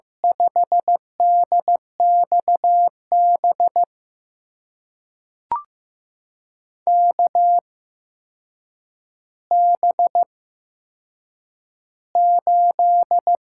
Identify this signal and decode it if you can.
Morse code, heard as a keyed tone, about 15 wpm, spaced as for 7 wpm, 700 Hz. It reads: 5DXB KB8
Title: USA Call Signs - 6 Characters - 8x Character Spacing - 15wpm